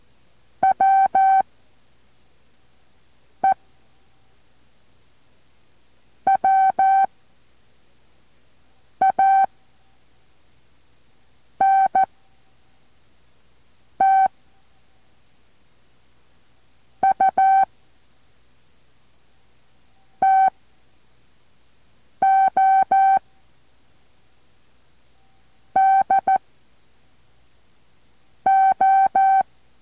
Morse code, heard as a keyed tone, about 14 wpm, spaced as for 4 wpm, 770 Hz. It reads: WE WANT U TO DO